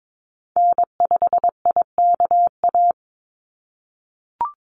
Morse code, heard as a keyed tone, about 22 wpm, 700 Hz. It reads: N5IKA
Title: USA Call Signs 22wpm